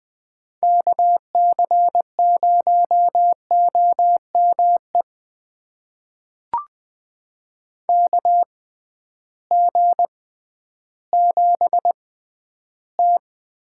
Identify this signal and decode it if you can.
Morse code, heard as a keyed tone, about 20 wpm, spaced as for 12 wpm, 700 Hz. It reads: KC0OME KG7T
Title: USA Call Signs - 6 Characters - 6x Character Spacing - 20wpm